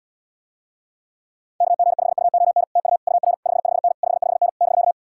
Morse code, heard as a keyed tone, about 79 wpm, 700 Hz. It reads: KG5UON TU UR 55N 55N <BK>